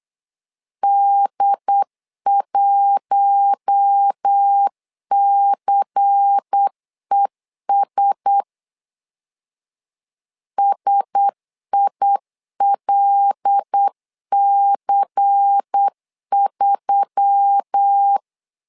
Morse code, heard as a keyed tone, about 8 wpm, 800 Hz. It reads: D1CES SILC3
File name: LEZIONI / 5 LBVC / 5 LBVC 8wpm 2935